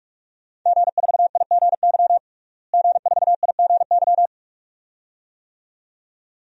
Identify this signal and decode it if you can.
Morse code, heard as a keyed tone, about 46 wpm, 700 Hz. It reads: G4IGY G4IGY
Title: DXCC-100 Call Signs - 5 Characters - Repeated 2x - 45wpm